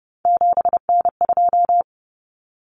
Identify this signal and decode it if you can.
Morse code, heard as a keyed tone, about 30 wpm, 700 Hz. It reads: 7N2
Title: Common Cut Numbers - 3-Digit 30wpm